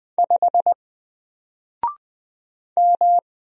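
Morse code, heard as a keyed tone, about 20 wpm, 700 Hz. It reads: 5 M
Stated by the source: Single letter and number Morse drill at 20 wpm, from N4LCD.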